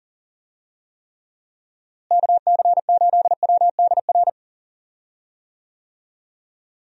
Morse code, heard as a keyed tone, about 38 wpm, 700 Hz. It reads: KC8WDR
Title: USA Call Signs - 6 Characters - Repeated 2x - 40wpm